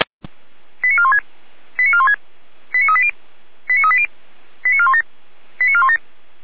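Five-tone selective call (ZVEI), heard as quick runs of five tones, tones of about 70 ms. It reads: 89416 89416 89380 89380 89417 89417
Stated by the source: ZVEI five-tone selective call received on an SDR.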